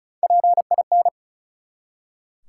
Morse code, heard as a keyed tone, about 34 wpm, 700 Hz. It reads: PIN
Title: Three Letter Words - 35wpm